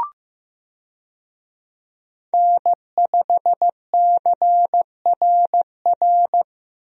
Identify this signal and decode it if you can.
Morse code, heard as a keyed tone, about 16 wpm, 700 Hz. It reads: N5CRR
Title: USA Call Signs - 5 Characters - Repeated 3x - 15wpm